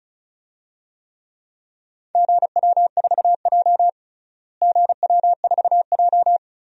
Morse code, heard as a keyed tone, about 34 wpm, 700 Hz. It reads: GW4J GW4J
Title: DXCC-100 Call Signs - 4 Characters - Repeated 2x - 35wpm